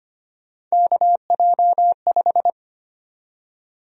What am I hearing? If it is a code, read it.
Morse code, heard as a keyed tone, about 25 wpm, 700 Hz. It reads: KJ5
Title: USA Call Sign Prefixes - 25wpm